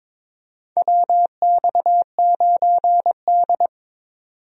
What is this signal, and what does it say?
Morse code, heard as a keyed tone, about 22 wpm, 700 Hz. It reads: WX9D
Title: USA Call Signs - 4 Characters - 22wpm